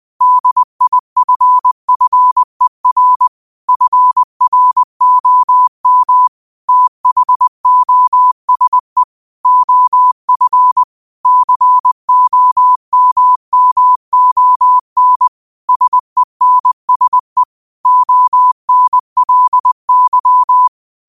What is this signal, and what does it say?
Morse code, heard as a keyed tone, about 20 wpm, 1000 Hz. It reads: DIFFER FROM THOSE OF COMMON SENSE ONLY
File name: qotdfiles / 20wpm